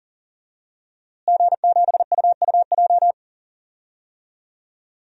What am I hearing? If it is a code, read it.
Morse code, heard as a keyed tone, about 39 wpm, 700 Hz. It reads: G7UUJ
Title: DXCC-100 Call Signs - 5 Characters - Repeated 2x - 40wpm